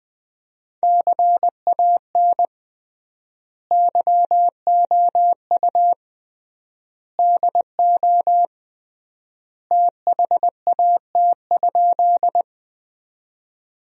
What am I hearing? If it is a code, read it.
Morse code, heard as a keyed tone, about 21 wpm, 700 Hz. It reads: CAN YOU DO THAT?
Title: 3x Word-Spacing - Sentences from Top 100 Words - 20wpm